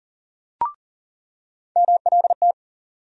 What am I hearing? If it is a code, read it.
Morse code, heard as a keyed tone, about 40 wpm, 700 Hz. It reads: MLT
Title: USA Call Sign Postfixes - 3 Letters 40wpm